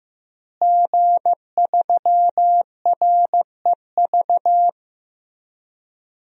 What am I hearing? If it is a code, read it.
Morse code, heard as a keyed tone, about 15 wpm, 700 Hz. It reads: G3REV
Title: DXCC-100 Call Signs - 5 Characters - Repeated 3x - 15wpm